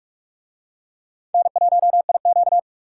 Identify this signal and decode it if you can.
Morse code, heard as a keyed tone, about 42 wpm, 700 Hz. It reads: N1IQ